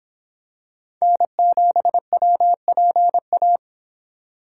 Morse code, heard as a keyed tone, about 26 wpm, 700 Hz. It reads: N7WPA